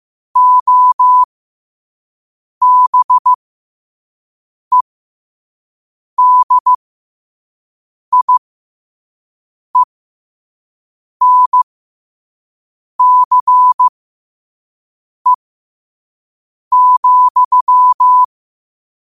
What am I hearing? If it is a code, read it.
Morse code, heard as a keyed tone, about 15 wpm, spaced as for 5 wpm, 1000 Hz. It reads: OBEDIENCE,
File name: qotdfiles / 5wpm